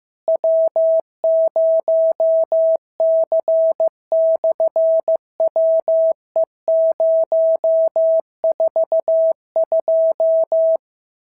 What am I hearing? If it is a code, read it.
Morse code, heard as a keyed tone, about 15 wpm, 650 Hz. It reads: W0C/WE042
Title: SOTA QSO Practice 15wpm